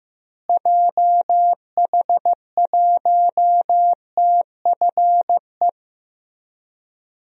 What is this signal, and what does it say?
Morse code, heard as a keyed tone, about 15 wpm, 700 Hz. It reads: JH1TFE
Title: DXCC-100 Call Signs - 6 Characters - Repeated 2x - 15wpm